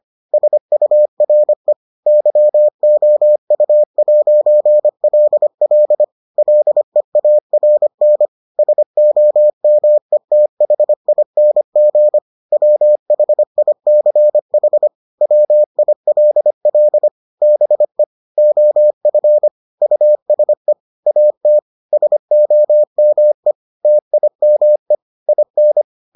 Morse code, heard as a keyed tone, about 25 wpm, 600 Hz. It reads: SURE YOU'LL LEARN SOMETHING WHICH WILL BE OF USE AT SOME TIME IN